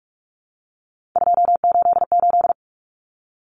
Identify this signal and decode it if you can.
Morse code, heard as a keyed tone, about 44 wpm, 700 Hz. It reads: 288
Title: Common Cut Numbers - 3-Digit 45wpm